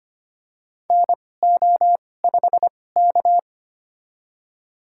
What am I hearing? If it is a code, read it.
Morse code, heard as a keyed tone, about 25 wpm, spaced as for 18 wpm, 700 Hz. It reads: NO5K
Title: USA Call Signs - 4 Characters - 2x Character Spacing - 25wpm